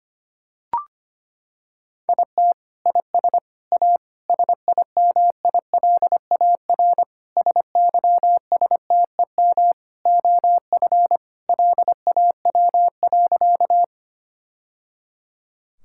Morse code, heard as a keyed tone, about 25 wpm, 700 Hz. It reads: IT IS A SIMILAR SYSTEM OF LAW.
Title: Sentences from Top 1000 Words - Review Part 5 - 25wpm